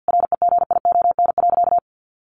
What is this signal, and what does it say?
Morse code, heard as a keyed tone, about 50 wpm, 700 Hz. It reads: REGION.
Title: Sentences from Top 1500 Words 50wpm